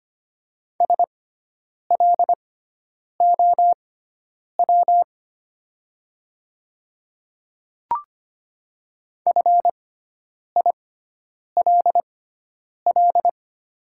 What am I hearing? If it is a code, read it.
Morse code, heard as a keyed tone, about 25 wpm, spaced as for 9 wpm, 700 Hz. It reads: SLOW FILL
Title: 6x Character Spacing - Top 500 Words - 25wpm